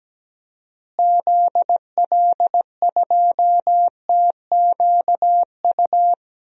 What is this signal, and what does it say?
Morse code, heard as a keyed tone, about 17 wpm, 700 Hz. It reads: ZL2TQU
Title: DXCC-100 Call Signs - 6 Characters - Repeated 2x - 17wpm